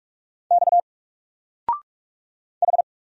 Morse code, heard as a keyed tone, about 44 wpm, 700 Hz. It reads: X H
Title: Single Letter-Number 45wpm